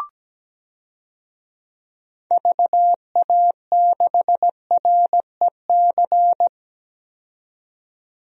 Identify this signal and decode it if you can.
Morse code, heard as a keyed tone, about 17 wpm, 700 Hz. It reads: VA6REC
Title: DXCC-100 Call Signs - 6 Characters - 17wpm